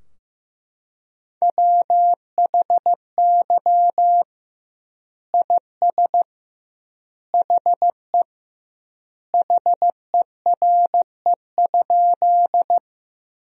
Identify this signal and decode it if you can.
Morse code, heard as a keyed tone, about 15 wpm, 700 Hz. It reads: WHY IS HE HERE?
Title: 2x Word-Spacing - Sentences from Top 200 Words - 15wpm